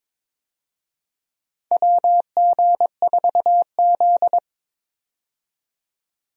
Morse code, heard as a keyed tone, about 21 wpm, 700 Hz. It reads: WG4Z
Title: USA Call Signs - 4 Characters - 22wpm